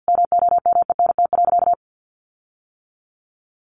Morse code, heard as a keyed tone, about 50 wpm, 700 Hz. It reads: MOMENT.
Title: Sentences from Top 1500 Words 50wpm